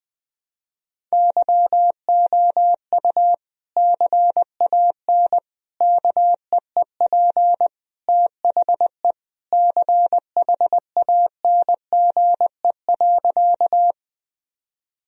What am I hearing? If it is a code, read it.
Morse code, heard as a keyed tone, about 20 wpm, 700 Hz. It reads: YOU CAN KEEP THE CHANGE.
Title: Sentences from Top 1000 Words - Review Part 1 - 20wpm